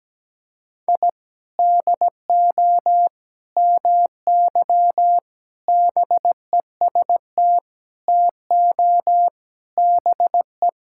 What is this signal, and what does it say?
Morse code, heard as a keyed tone, about 17 wpm, 700 Hz. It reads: I DO MY BEST TO BE
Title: Sentences from Top 1000 Words - Review Part 6 - 17wpm